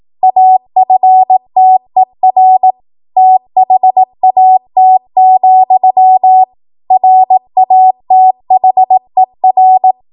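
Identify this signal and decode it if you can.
Morse code, heard as a keyed tone, about 18 wpm, 750 Hz. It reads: AFTER THAT, RATHER
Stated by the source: ARRL 18 wpm Morse code practice archive transmission.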